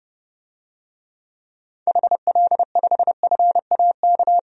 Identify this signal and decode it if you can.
Morse code, heard as a keyed tone, about 30 wpm, 700 Hz. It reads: HL5FAK
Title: DXCC-100 Call Signs - 6 Characters - 30wpm